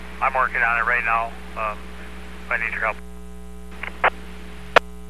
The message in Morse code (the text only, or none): none